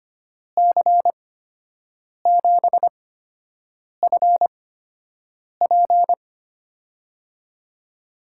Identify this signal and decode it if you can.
Morse code, heard as a keyed tone, about 25 wpm, spaced as for 7 wpm, 700 Hz. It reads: C7FP
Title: USA Call Signs - 5 Characters - 8x Character Spacing - 25wpm